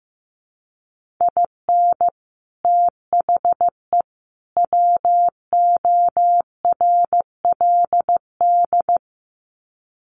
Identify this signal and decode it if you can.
Morse code, heard as a keyed tone, about 15 wpm, 700 Hz. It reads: IN THE WORLD
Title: Sets of 3 Words 15wpm